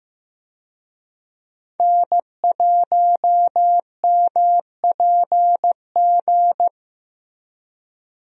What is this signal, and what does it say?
Morse code, heard as a keyed tone, about 15 wpm, 700 Hz. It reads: N1MPG
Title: USA Call Signs - 5 Characters - Repeated 2x - 15wpm